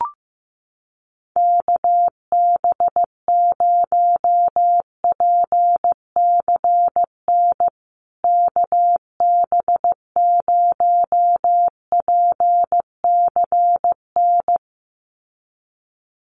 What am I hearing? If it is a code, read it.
Morse code, heard as a keyed tone, about 15 wpm, 700 Hz. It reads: KB0PCN KB0PCN
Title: USA Call Signs Repeated 2x 15wpm